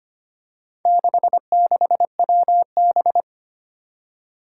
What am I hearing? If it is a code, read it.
Morse code, heard as a keyed tone, about 25 wpm, 700 Hz. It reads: 66WB
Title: Four Challenging Letter-Number Combinations 25wpm